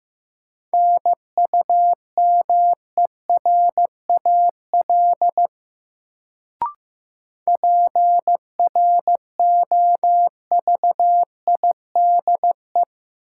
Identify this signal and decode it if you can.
Morse code, heard as a keyed tone, about 15 wpm, 700 Hz. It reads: NUMERAL PROVIDE
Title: Sentences from Top 1000 Words - Review Part 2 - New Word Drill - 15wpm